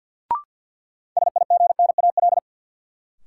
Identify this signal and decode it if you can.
Morse code, heard as a keyed tone, about 53 wpm, 700 Hz. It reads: SIGNAL
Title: Sentences from Top 1400 Words - New Word Drill Alt - 50wpm